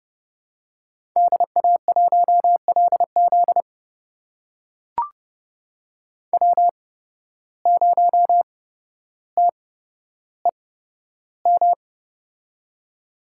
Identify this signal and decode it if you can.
Morse code, heard as a keyed tone, about 29 wpm, spaced as for 13 wpm, 700 Hz. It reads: DA1LZ W0TEM
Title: USA Call Signs - 8x Character Spacing - 30wpm